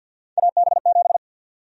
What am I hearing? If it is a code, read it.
Morse code, heard as a keyed tone, about 48 wpm, 700 Hz. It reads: AB7
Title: USA Call Sign Prefixes - 50wpm